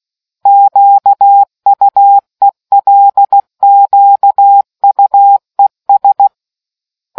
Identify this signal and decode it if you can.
Morse code, heard as a keyed tone, about 16 wpm, 800 Hz. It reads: QUELQUES